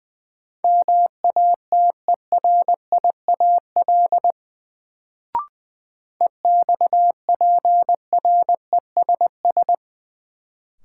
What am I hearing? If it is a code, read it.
Morse code, heard as a keyed tone, about 20 wpm, 700 Hz. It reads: MATERIAL EXPRESS
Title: Sentences from Top 1100 Words - New Word Drill Alt - 20wpm